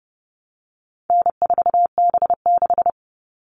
Morse code, heard as a keyed tone, about 30 wpm, 700 Hz. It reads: N4B6
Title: All Cut Numbers - 4-Digit 30wpm